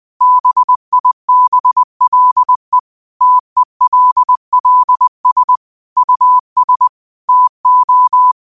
Morse code, heard as a keyed tone, about 20 wpm, 1000 Hz. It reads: BIBLE TELLS US TO
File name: qotdfiles / 20wpm